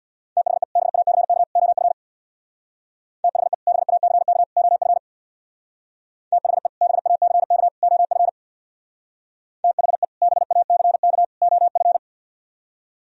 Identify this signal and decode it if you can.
Morse code, heard as a keyed tone, about 64 wpm, 700 Hz. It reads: THE BACK OF THE BACK OF THE BACK OF THE BACK OF